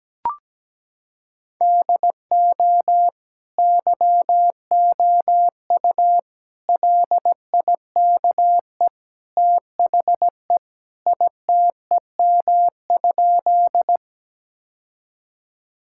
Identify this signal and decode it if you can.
Morse code, heard as a keyed tone, about 17 wpm, 700 Hz. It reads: DO YOU LIKE THE ITEM?
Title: Sentences from Top 1000 Words - Review Part 6 - 17wpm